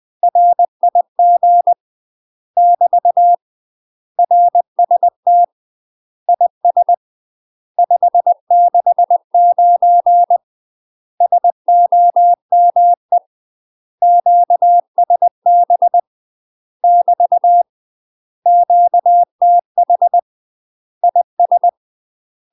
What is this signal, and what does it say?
Morse code, heard as a keyed tone, about 20 wpm, 700 Hz. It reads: RIG <BT> RST IS 569 SOME QSB <BT> QTH IS
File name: UK MASTER QSOs TRAINING / 20WPM / UK MASTER QSOs 0065